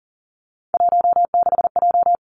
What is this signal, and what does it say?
Morse code, heard as a keyed tone, about 41 wpm, 700 Hz. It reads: 16J
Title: Three Challenging Letter-Number Combinations 40wpm